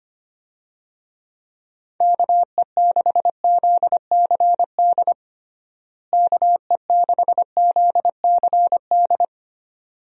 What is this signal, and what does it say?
Morse code, heard as a keyed tone, about 25 wpm, 700 Hz. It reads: KE6ZCD KE6ZCD
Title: USA Call Signs - 6 Characters - Repeated 2x - 25wpm